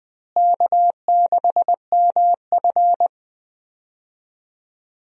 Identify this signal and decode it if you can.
Morse code, heard as a keyed tone, about 21 wpm, 700 Hz. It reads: K6MF
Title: USA Call Signs - 4 Characters - Repeated 3x - 20wpm